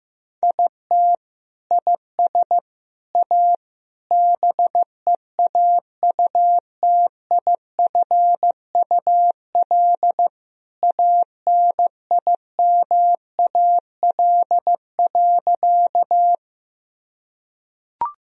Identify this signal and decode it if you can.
Morse code, heard as a keyed tone, about 15 wpm, 700 Hz. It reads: IT IS A BEAUTIFUL ANIMAL.
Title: Sentences from Top 1000 Words - Review Part 1 - 15wpm